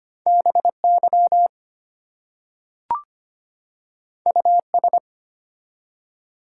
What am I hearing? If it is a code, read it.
Morse code, heard as a keyed tone, about 25 wpm, 700 Hz. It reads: BY US
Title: Two Letter Words - 25wpm